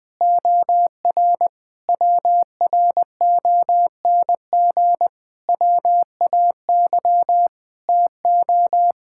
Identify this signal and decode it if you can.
Morse code, heard as a keyed tone, about 21 wpm, 700 Hz. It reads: OR WRONG WAY TO
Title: Sentences with 8-Word Phrases - 20wpm